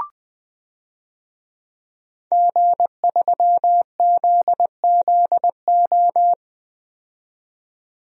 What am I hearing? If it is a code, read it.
Morse code, heard as a keyed tone, about 20 wpm, 700 Hz. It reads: G3ZZO
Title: DXCC-100 Call Signs - 5 Characters - 20wpm